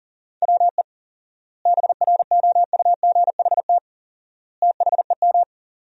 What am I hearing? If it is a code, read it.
Morse code, heard as a keyed tone, about 40 wpm, 700 Hz. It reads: WE BROUGHT THEM
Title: 4x Word-Spacing - Sentences from Top 500 Words - 40wpm